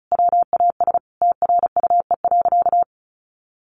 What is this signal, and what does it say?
Morse code, heard as a keyed tone, about 34 wpm, 700 Hz. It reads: WAS TRUE.
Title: Sentences from Top 400 Words 35wpm